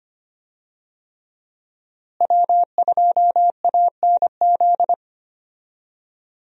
Morse code, heard as a keyed tone, about 25 wpm, 700 Hz. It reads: W2ANZ
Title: USA Call Signs - 5 Characters - Repeated 3x - 25wpm